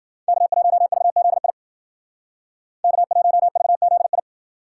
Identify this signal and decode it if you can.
Morse code, heard as a keyed tone, about 51 wpm, 700 Hz. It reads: K1VZI K1VZI